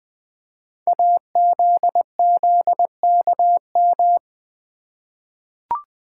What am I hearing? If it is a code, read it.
Morse code, heard as a keyed tone, about 20 wpm, 700 Hz. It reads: AZZKM
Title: Code Groups - 5 Letters - 20wpm